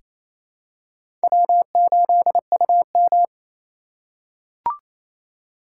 Morse code, heard as a keyed tone, about 28 wpm, 700 Hz. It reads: W8UM
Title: USA Call Signs - 4x Character Spacing - 28wpm